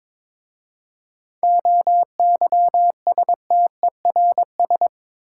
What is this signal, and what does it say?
Morse code, heard as a keyed tone, about 23 wpm, 690 Hz. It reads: OYSTERS
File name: Difficult Words with Hints - 22wpm